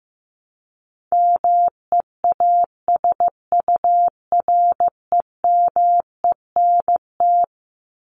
Morse code, heard as a keyed tone, about 15 wpm, 700 Hz. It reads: MEASUREMENT
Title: Top 1751-2000 Words 15wpm